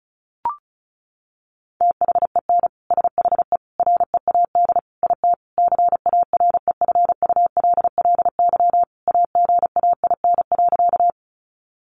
Morse code, heard as a keyed tone, about 34 wpm, 700 Hz. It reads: THEN SHE READ IT CAREFULLY AGAIN.
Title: Sentences from Top 1300 Words 35wpm